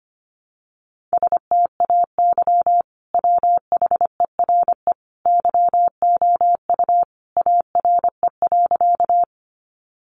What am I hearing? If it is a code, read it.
Morse code, heard as a keyed tone, about 25 wpm, 700 Hz. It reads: STAY WHERE YOU ARE.
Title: Sentences from Top 500 Words 25wpm